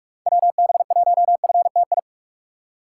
Morse code, heard as a keyed tone, about 46 wpm, 700 Hz. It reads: WB1FTI
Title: USA Call Signs - 6 Characters - 3x Character Spacing - 45wpm